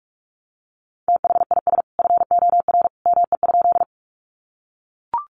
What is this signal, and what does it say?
Morse code, heard as a keyed tone, about 44 wpm, 700 Hz. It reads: THIS FOR ME?
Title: Sentences from Top 200 Words 45wpm